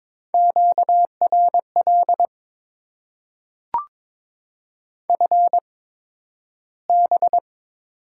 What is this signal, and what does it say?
Morse code, heard as a keyed tone, about 22 wpm, spaced as for 12 wpm, 700 Hz. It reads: QRL FB
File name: Top 20 QSO Elements - 8x Character Spacing - 22wpm